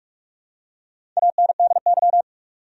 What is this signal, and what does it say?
Morse code, heard as a keyed tone, about 46 wpm, 700 Hz. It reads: ANDY
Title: USA Common Names - Max 4 Letters - 45wpm